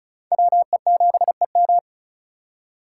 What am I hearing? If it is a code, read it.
Morse code, heard as a keyed tone, about 36 wpm, 700 Hz. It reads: WE7EM
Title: USA Call Signs - 5 Characters - 4x Character Spacing - 35wpm